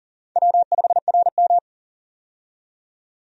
Morse code, heard as a keyed tone, about 40 wpm, 700 Hz. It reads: W5RM
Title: USA Call Signs - 4 Characters - 40wpm